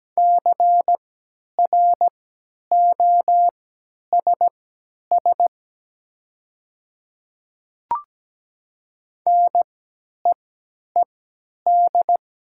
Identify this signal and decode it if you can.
Morse code, heard as a keyed tone, about 17 wpm, spaced as for 10 wpm, 690 Hz. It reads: CROSS NEED
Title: 3x Character Spacing - Top 300 Words - 17wpm